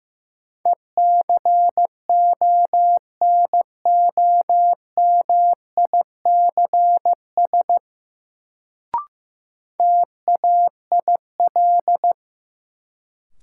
Morse code, heard as a keyed tone, about 15 wpm, 700 Hz. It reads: ECONOMICS TAIL
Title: Sentences from Top 1000 Words - Review Part 3 - New Word Drill - 15wpm